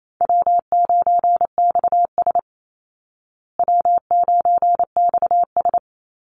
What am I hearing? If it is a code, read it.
Morse code, heard as a keyed tone, about 28 wpm, 700 Hz. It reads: W9XS W9XS
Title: Easy Call Signs Repeated 2x - 28wpm